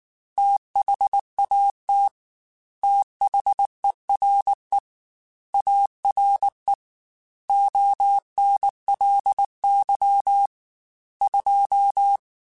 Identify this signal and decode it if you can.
Morse code, heard as a keyed tone, about 19 wpm, 800 Hz. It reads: THAT THERE ARE ONLY 2
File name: Technical Subjects / Technical Subjects-mp3s / AND Gate @18